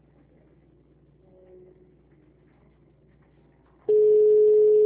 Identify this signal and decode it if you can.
Ringback tone (Europe, 425 Hz).
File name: out- -2005- - - .38